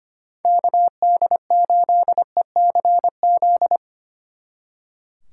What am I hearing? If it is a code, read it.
Morse code, heard as a keyed tone, about 33 wpm, 700 Hz. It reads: KD8ECZ